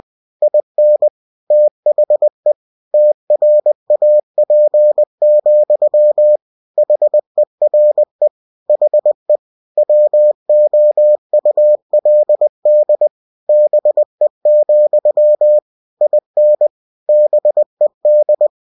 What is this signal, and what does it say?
Morse code, heard as a keyed tone, about 20 wpm, 600 Hz. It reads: IN THE TRAP, HERE HE WOULD BE, IN BED
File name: cw-books / winnie-the-pooh / winnie-20wpm-page-0041